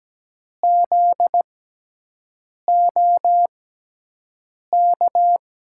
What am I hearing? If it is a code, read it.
Morse code, heard as a keyed tone, about 17 wpm, spaced as for 6 wpm, 700 Hz. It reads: ZOK
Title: USA Call Signs - 6 Characters - 6x Character Spacing - 17wpm